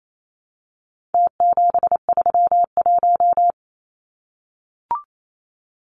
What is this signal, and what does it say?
Morse code, heard as a keyed tone, about 28 wpm, 700 Hz. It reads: T731
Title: Numbers - Contest Style - 4-Digit 28wpm